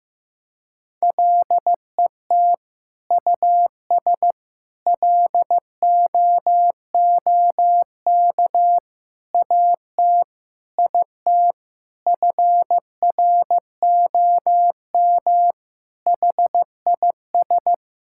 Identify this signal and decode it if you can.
Morse code, heard as a keyed tone, about 15 wpm, 700 Hz. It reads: LET US LOOK AT IT FROM HIS
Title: Sentences from Top 1000 Words - Review Part 6 - 15wpm